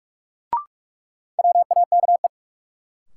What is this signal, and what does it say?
Morse code, heard as a keyed tone, about 42 wpm, 700 Hz. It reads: WAKE
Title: Sets of 4 Words - New Word Drill - 45wpm